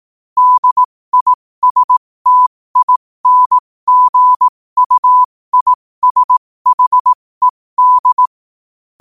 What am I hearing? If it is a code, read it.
Morse code, heard as a keyed tone, about 18 wpm, spaced as for 15 wpm, 1000 Hz. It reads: DISTINGUISHED
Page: New quotes every day in morse code at 15 Words per minute.